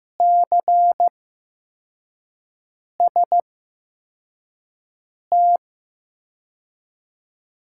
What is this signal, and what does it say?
Morse code, heard as a keyed tone, about 15 wpm, spaced as for 4 wpm, 700 Hz. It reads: CST
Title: USA Call Signs - 6 Characters - 8x Character Spacing - 15wpm